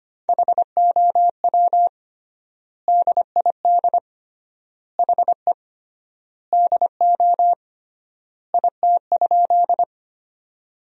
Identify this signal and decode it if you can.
Morse code, heard as a keyed tone, about 25 wpm, 700 Hz. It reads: HOW DID HE DO IT?
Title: 3x Word-Spacing - Sentences from Top 100 Words - 25wpm